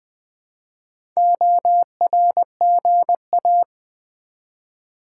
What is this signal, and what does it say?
Morse code, heard as a keyed tone, about 20 wpm, 700 Hz. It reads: ORGA
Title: Build A Word - 1 Letter At A Time - 3 to 5-Letter Words - 20wpm